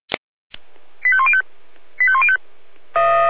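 Five-tone selective call (ZVEI), heard as quick runs of five tones, tones of about 70 ms. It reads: 85105 85105
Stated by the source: ZVEI five-tone selective call 85310 received on an SDR.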